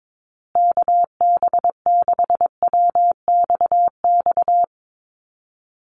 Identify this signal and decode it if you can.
Morse code, heard as a keyed tone, about 22 wpm, 700 Hz. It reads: KB6WXX
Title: USA Call Signs Repeated 2x 22wpm